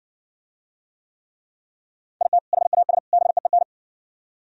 Morse code, heard as a keyed tone, about 59 wpm, 700 Hz. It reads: IT HAS BEEN